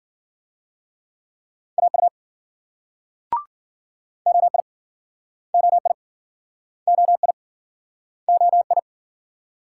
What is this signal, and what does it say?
Morse code, heard as a keyed tone, about 58 wpm, 700 Hz. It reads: AU OI OI OI OI